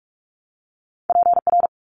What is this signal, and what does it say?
Morse code, heard as a keyed tone, about 46 wpm, 700 Hz. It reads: PR
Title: ARRL Sections 45wpm